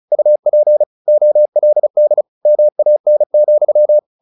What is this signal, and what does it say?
Morse code, heard as a keyed tone, about 34 wpm, 600 Hz. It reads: UP OLD MAN,
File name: the old man and the sea / the old man and the sea 1 35wpm